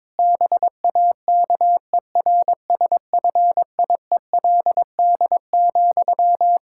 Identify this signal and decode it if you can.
Morse code, heard as a keyed tone, about 22 wpm, 700 Hz. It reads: BAKERSFIELD,
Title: Practice QSO - Exchange 1 - Beginning - 22wpm